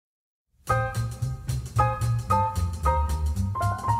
There is no code.